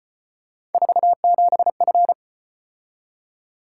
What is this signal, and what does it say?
Morse code, heard as a keyed tone, about 32 wpm, 700 Hz. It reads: 47F